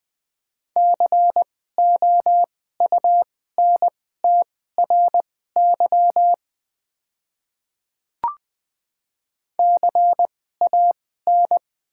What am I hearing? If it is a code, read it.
Morse code, heard as a keyed tone, about 20 wpm, spaced as for 14 wpm, 700 Hz. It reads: COUNTRY CAN
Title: 2x Character Spacing - Top 300 Words - 20wpm